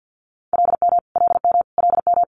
Morse code, heard as a keyed tone, about 50 wpm, 700 Hz. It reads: LM LM LM
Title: Multi-letter Phonemes Repeated 3x 50wpm